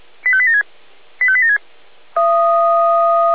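Five-tone selective call (ZVEI), heard as quick runs of five tones, tones of about 70 ms. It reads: 85876 85876
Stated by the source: ZVEI five-tone selective call received by SDR.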